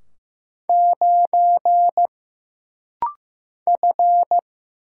Morse code, heard as a keyed tone, about 15 wpm, 700 Hz. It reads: 9 F
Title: Single Letter-Number Alt - Rapid-Fire 15wpm